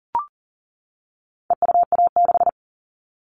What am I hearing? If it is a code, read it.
Morse code, heard as a keyed tone, about 37 wpm, 700 Hz. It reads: EUA6
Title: All Cut Numbers - 4-Digit 40wpm